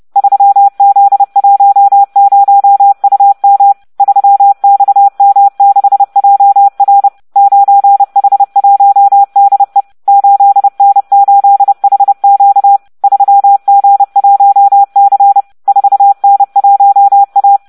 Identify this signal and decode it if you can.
Morse code, heard as a keyed tone, about 30 wpm, 800 Hz. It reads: 3Z10UM 3XM6JR 9H1DE 8N8HQ 3G1C 4N1A